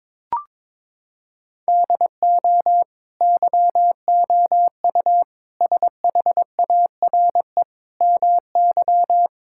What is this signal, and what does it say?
Morse code, heard as a keyed tone, about 22 wpm, 700 Hz. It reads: DO YOU SHARE MY
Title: Sentences from Top 1600 Words 22wpm